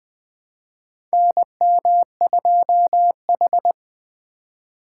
Morse code, heard as a keyed tone, about 20 wpm, 700 Hz. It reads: NM2H
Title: USA Call Signs - 4 Characters - 6x Character Spacing - 20wpm